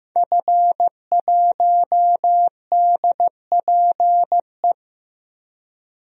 Morse code, heard as a keyed tone, about 15 wpm, 700 Hz. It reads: F1DPE
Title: DXCC-100 Call Signs - 5 Characters - Repeated 3x - 15wpm